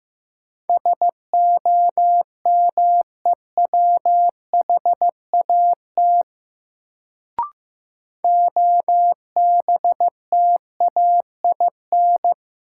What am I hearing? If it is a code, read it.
Morse code, heard as a keyed tone, about 15 wpm, 700 Hz. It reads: SOMEWHAT OBTAIN
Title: Sentences from Top 1300 Words - New Word Drill Alt - 15wpm